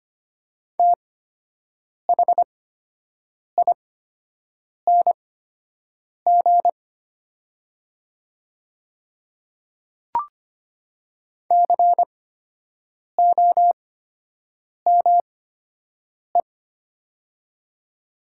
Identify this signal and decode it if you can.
Morse code, heard as a keyed tone, about 25 wpm, spaced as for 7 wpm, 700 Hz. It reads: THING COME
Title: 8x Character Spacing - Top 100 Words - 25wpm